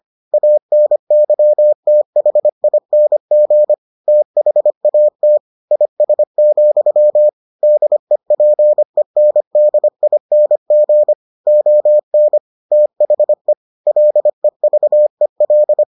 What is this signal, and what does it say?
Morse code, heard as a keyed tone, about 25 wpm, 600 Hz. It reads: ANYTHING THAT IS, DEPENDING ON THE LEVEL